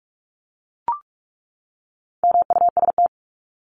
Morse code, heard as a keyed tone, about 44 wpm, 700 Hz. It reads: MUST
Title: Top 500 Words 45wpm